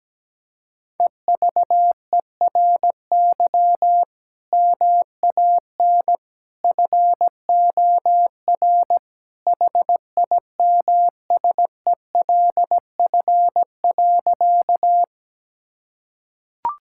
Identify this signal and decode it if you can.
Morse code, heard as a keyed tone, about 17 wpm, 700 Hz. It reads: EVERY MAN FOR HIMSELF.
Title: Sentences from Top 1600 Words 17wpm